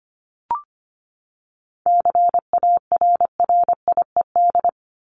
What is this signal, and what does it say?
Morse code, heard as a keyed tone, about 25 wpm, 700 Hz. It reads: CARRIED
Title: ED Suffix - Top 100 Words with Base Words 25wpm